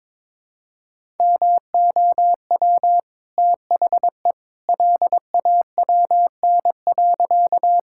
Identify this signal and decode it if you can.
Morse code, heard as a keyed tone, about 22 wpm, 690 Hz. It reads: MOW THE LAWN.
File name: practice / 22wpm / Easy Sentences with Hints - 22wpm